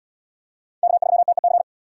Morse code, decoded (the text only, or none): K3TEC